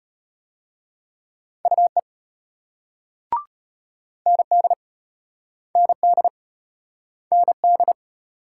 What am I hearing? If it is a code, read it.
Morse code, heard as a keyed tone, about 34 wpm, 700 Hz. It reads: UE ND ND ND